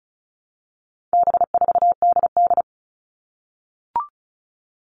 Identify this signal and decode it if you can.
Morse code, heard as a keyed tone, about 34 wpm, 700 Hz. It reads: B4DD
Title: All Cut Numbers - 4-Digit 35wpm